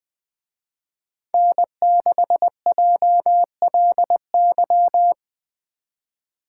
Morse code, heard as a keyed tone, about 20 wpm, 700 Hz. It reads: N6JLY